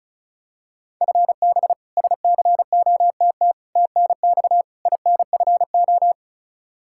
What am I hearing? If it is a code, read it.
Morse code, heard as a keyed tone, about 34 wpm, 700 Hz. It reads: FB SCOTT TNX INFO
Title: K1USN SST QSO Practice 35wpm